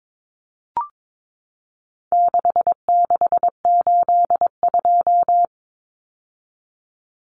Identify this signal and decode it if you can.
Morse code, heard as a keyed tone, about 22 wpm, 700 Hz. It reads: 6682
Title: Numbers - 4-Digit 22wpm